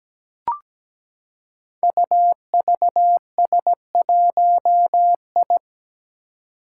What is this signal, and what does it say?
Morse code, heard as a keyed tone, about 17 wpm, 700 Hz. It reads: UVS1I
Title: Five Challenging Letter-Number Combinations 17wpm